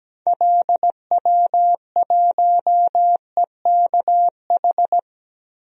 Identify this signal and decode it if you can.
Morse code, heard as a keyed tone, about 17 wpm, 700 Hz. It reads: LW1EKH
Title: DXCC-100 Call Signs - 6 Characters - Repeated 2x - 17wpm